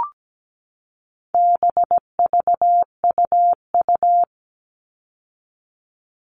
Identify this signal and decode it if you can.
Morse code, heard as a keyed tone, about 17 wpm, 700 Hz. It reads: BVUU
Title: All Cut Numbers - 4-Digit 17wpm